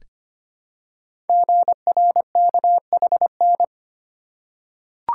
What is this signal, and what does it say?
Morse code, heard as a keyed tone, about 25 wpm, 700 Hz. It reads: GRKHN
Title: Code Groups - 5 Letter:Numbers - 25wpm